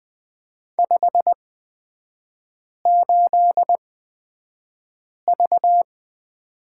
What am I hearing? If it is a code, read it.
Morse code, heard as a keyed tone, about 20 wpm, spaced as for 5 wpm, 690 Hz. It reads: 58V
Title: 229 Lesson 37e - Letter x - Three Characters 20wpm